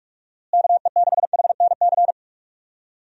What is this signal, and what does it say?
Morse code, heard as a keyed tone, about 41 wpm, 700 Hz. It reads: KE6HNC
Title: USA Call Signs - 6 Characters - Repeated 2x - 45wpm